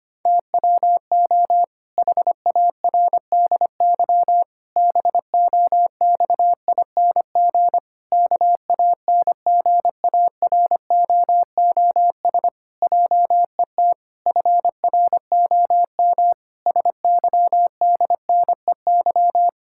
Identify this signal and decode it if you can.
Morse code, heard as a keyed tone, about 25 wpm, 700 Hz. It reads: TWO HARDY BOXING KANGAROOS JET FROM SYDNEY
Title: Pangram Challenge v2 - 25wpm